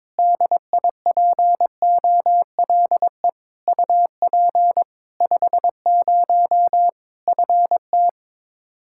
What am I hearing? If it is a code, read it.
Morse code, heard as a keyed tone, about 22 wpm, 700 Hz. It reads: DIPOLE UP 50 FT
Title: Practice QSO - Exchange 2 - Info - 22wpm